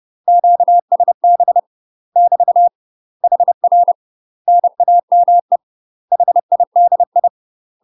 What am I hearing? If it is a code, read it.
Morse code, heard as a keyed tone, about 30 wpm, 700 Hz. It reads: QSB <BT> HR NAME HIDI
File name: UK MASTER QSOs TRAINING / 30WPM / UK MASTER QSOs 0038